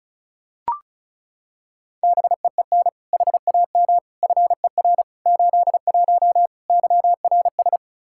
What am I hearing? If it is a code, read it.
Morse code, heard as a keyed tone, about 36 wpm, 700 Hz. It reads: BEEN HAM FER 81 YRS
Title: Practice QSO - Exchange 3 - More Info - 35wpm